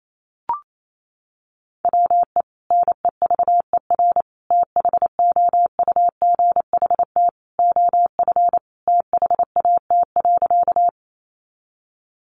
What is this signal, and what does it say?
Morse code, heard as a keyed tone, about 28 wpm, 700 Hz. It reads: WE NEVER THOUGHT OF THAT.
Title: Sentences from Top 300 Words 28wpm